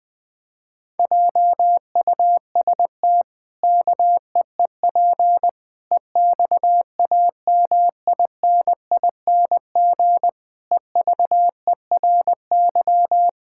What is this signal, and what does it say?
Morse code, heard as a keyed tone, about 20 wpm, 700 Hz. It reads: JUST KEEP EXAMINING EVERY
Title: Pangram Challenge v2 - 20wpm